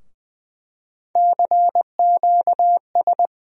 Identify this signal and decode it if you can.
Morse code, heard as a keyed tone, about 21 wpm, 700 Hz. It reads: CQS
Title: USA Call Sign Postfixes - 3 Letters 20wpm